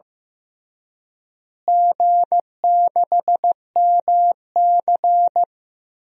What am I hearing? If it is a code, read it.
Morse code, heard as a keyed tone, about 15 wpm, 700 Hz. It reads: G6MC
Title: DXCC-100 Call Signs - 4 Characters - Repeated 3x - 15wpm